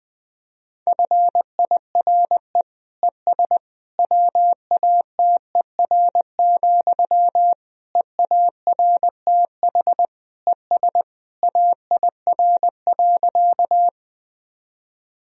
Morse code, heard as a keyed tone, about 20 wpm, 700 Hz. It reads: FIRE ES WATER, EARTH ES AIR.
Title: Sentences from Top 400 Words and Abbreviations 20wpm